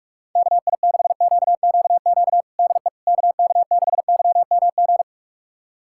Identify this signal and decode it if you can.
Morse code, heard as a keyed tone, about 46 wpm, 700 Hz. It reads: KI6QQQ DE KK6YMG
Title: USA Callsign DE Callsign 45wpm